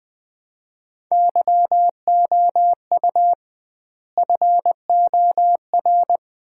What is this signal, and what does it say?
Morse code, heard as a keyed tone, about 21 wpm, 700 Hz. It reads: YOU FOR
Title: 2x Word-Spacing - Sets of 5 Words - 20wpm